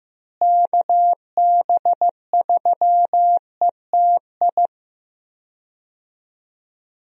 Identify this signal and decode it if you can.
Morse code, heard as a keyed tone, about 16 wpm, 700 Hz. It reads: KB3ETI
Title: USA Call Signs - 6 Characters - Repeated 3x - 15wpm